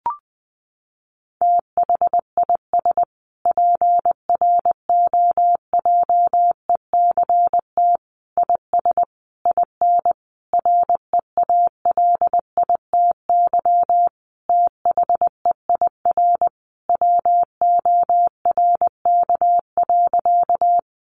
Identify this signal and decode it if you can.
Morse code, heard as a keyed tone, about 20 wpm, 700 Hz. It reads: THIS PROJECT IS IN REALITY THEIR WORK.
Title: Sentences from Top 1400 Words 20wpm